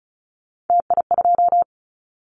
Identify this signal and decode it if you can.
Morse code, heard as a keyed tone, about 36 wpm, 700 Hz. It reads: TI2
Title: Callsign Prefixes for Top 100 DXCC - 35wpm